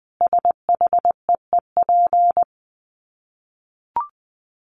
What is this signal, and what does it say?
Morse code, heard as a keyed tone, about 20 wpm, 700 Hz. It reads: SHEEP
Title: Dit Challenge 20wpm